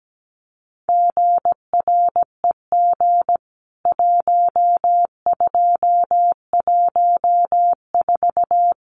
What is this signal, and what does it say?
Morse code, heard as a keyed tone, about 17 wpm, 700 Hz. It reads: GREG 1214
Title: CWT Exchange 17wpm